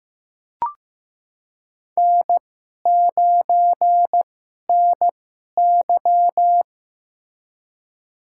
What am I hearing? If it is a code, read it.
Morse code, heard as a keyed tone, about 15 wpm, spaced as for 11 wpm, 700 Hz. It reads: N9NY
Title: USA Call Signs - 4 Characters - 2x Character Spacing - 15wpm